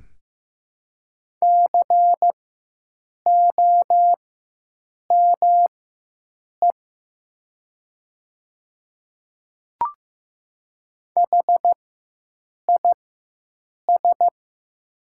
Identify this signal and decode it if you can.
Morse code, heard as a keyed tone, about 15 wpm, spaced as for 7 wpm, 700 Hz. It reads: COME HIS
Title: 4x Character Spacing - Top 100 Words - 15wpm